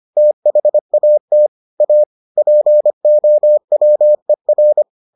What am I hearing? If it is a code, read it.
Morse code, heard as a keyed tone, about 25 wpm, 600 Hz. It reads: THAT A POWER